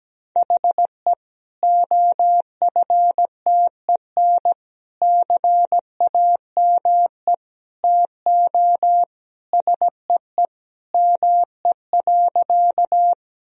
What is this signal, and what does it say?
Morse code, heard as a keyed tone, about 17 wpm, 700 Hz. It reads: HE OFTEN CAME TO SEE ME.
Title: Sentences from Top 1000 Words - Review Part 1 - 17wpm